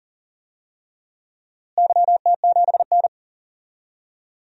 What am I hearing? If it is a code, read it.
Morse code, heard as a keyed tone, about 40 wpm, 700 Hz. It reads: YT7N
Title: DXCC-100 Call Signs - 4 Characters - Repeated 3x - 40wpm